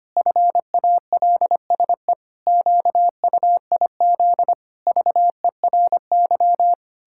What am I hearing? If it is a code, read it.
Morse code, heard as a keyed tone, about 25 wpm, 700 Hz. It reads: FALSE QUIZ VERY
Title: Pangram Challenge v2 - 25wpm